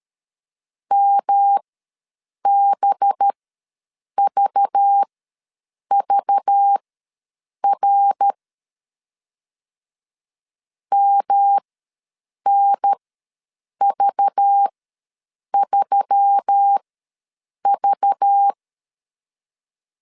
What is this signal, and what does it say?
Morse code, heard as a keyed tone, about 13 wpm, spaced as for 7 wpm, 800 Hz. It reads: MBVVR MNV3V
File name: LEZIONI / 5 LBVC / 5 LBV ws 12wpm 3015